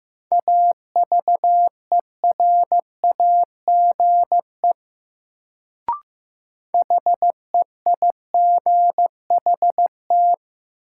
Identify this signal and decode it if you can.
Morse code, heard as a keyed tone, about 15 wpm, 700 Hz. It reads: AVERAGE HEIGHT
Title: Sets of 5 Words - New Word Drill - 15wpm